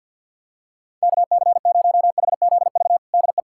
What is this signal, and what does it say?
Morse code, heard as a keyed tone, about 47 wpm, 700 Hz. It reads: KC0HZV DE
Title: USA Callsign DE Callsign - 6 Characters - 50wpm